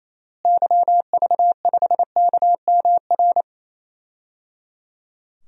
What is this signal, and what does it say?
Morse code, heard as a keyed tone, about 28 wpm, 700 Hz. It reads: YV5KMR
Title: DXCC-100 Call Signs - 6 Characters - Repeated 3x - 28wpm